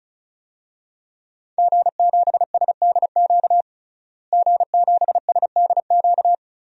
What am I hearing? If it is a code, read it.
Morse code, heard as a keyed tone, about 34 wpm, 700 Hz. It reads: G7SDQ G7SDQ
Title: DXCC-100 Call Signs - 5 Characters - Repeated 3x - 35wpm